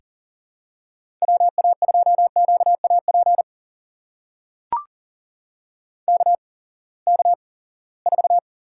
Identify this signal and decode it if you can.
Morse code, heard as a keyed tone, about 42 wpm, spaced as for 23 wpm, 700 Hz. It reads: WA2QAP KK4
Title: USA Call Signs - 6 Characters - 8x Character Spacing - 40wpm